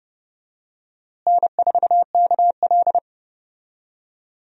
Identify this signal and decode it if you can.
Morse code, heard as a keyed tone, about 30 wpm, 700 Hz. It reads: N4KL
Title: USA Call Signs - 4 Characters - Repeated 2x - 30wpm